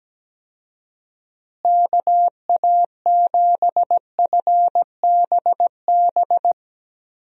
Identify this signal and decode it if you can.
Morse code, heard as a keyed tone, about 17 wpm, 700 Hz. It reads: KA7FBB
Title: USA Call Signs - 6 Characters - Repeated 2x - 17wpm